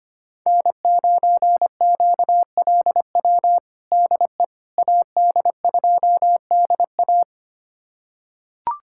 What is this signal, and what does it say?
Morse code, heard as a keyed tone, about 25 wpm, 700 Hz. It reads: N9QLW DE AD2DA
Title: USA Callsign DE Callsign - 5 Characters - 25wpm